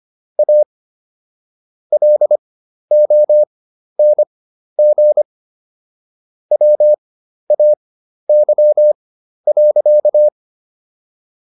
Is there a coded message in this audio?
Yes: Morse code from 0.4 to 10.3 s, heard as a keyed tone, about 25 wpm, spaced as for 12 wpm, 600 Hz. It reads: A LONG WAY.